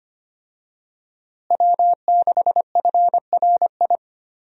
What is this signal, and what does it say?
Morse code, heard as a keyed tone, about 25 wpm, 700 Hz. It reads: W6FRI
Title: USA Call Signs - 5 Characters - Repeated 3x - 25wpm